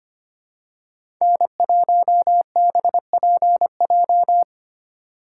Morse code, heard as a keyed tone, about 25 wpm, 700 Hz. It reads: N1BPJ
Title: USA Call Signs - 3x Character Spacing - 25wpm